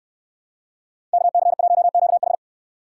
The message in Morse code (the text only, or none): KC1QH